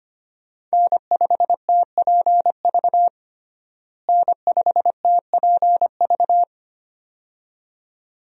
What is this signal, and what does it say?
Morse code, heard as a keyed tone, about 25 wpm, 700 Hz. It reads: N5TPV N5TPV
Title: USA Call Signs - 5 Characters - Repeated 2x - 25wpm